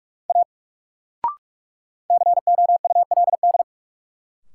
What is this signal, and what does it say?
Morse code, heard as a keyed tone, about 46 wpm, 700 Hz. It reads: A COULD
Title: Sentences from Top 100 Words - New Word Drill - 45wpm